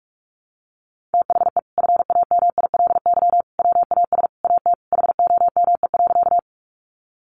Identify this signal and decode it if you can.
Morse code, heard as a keyed tone, about 46 wpm, 700 Hz. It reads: THE FAMILY WAS AT HOME.
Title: Sentences from Top 400 Words 45wpm